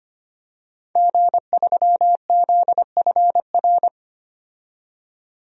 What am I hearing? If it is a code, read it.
Morse code, heard as a keyed tone, about 25 wpm, 700 Hz. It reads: G3ZFR
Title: DXCC-100 Call Signs - 5 Characters - 25wpm